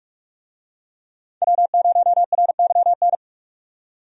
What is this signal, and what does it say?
Morse code, heard as a keyed tone, about 43 wpm, 700 Hz. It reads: W0RYN
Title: USA Call Signs - 5 Characters - 45wpm